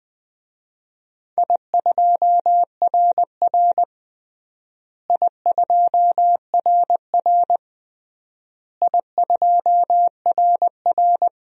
Morse code, heard as a keyed tone, about 20 wpm, 700 Hz. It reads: I2RR I2RR I2RR
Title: DXCC-100 Call Signs - 4 Characters - Repeated 3x - 20wpm